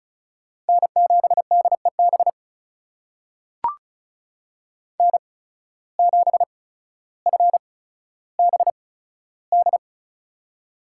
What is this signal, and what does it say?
Morse code, heard as a keyed tone, about 36 wpm, spaced as for 15 wpm, 700 Hz. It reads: N7DEB N7FBD
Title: USA Call Signs - 5 Characters - 8x Character Spacing - 35wpm